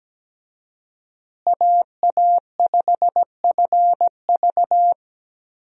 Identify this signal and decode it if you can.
Morse code, heard as a keyed tone, about 17 wpm, 700 Hz. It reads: AA5FV
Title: USA Call Signs - 5 Characters - Repeated 2x - 17wpm